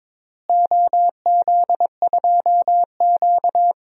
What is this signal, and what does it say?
Morse code, heard as a keyed tone, about 22 wpm, 700 Hz. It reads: OZ2Q
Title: DXCC-100 Call Signs - 4 Characters - Repeated 3x - 22wpm